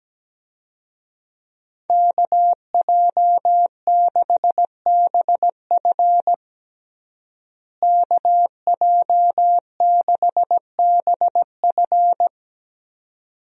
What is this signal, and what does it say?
Morse code, heard as a keyed tone, about 17 wpm, 700 Hz. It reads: KJ6BF KJ6BF
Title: USA Call Signs - 5 Characters - Repeated 2x - 17wpm